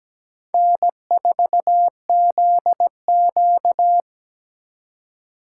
Morse code, heard as a keyed tone, about 20 wpm, 700 Hz. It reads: N4ZQ
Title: Speed Racing - USA 1x2 Call Signs 17wpm